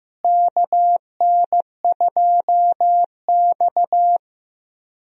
Morse code, heard as a keyed tone, about 15 wpm, 700 Hz. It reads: KN2X
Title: USA Call Signs - 4 Characters - Repeated 3x - 15wpm